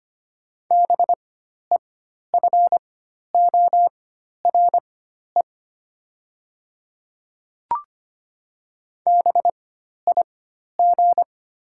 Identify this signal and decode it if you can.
Morse code, heard as a keyed tone, about 25 wpm, spaced as for 12 wpm, 700 Hz. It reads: BEFORE BIG
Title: 4x Character Spacing - Top 300 Words - 25wpm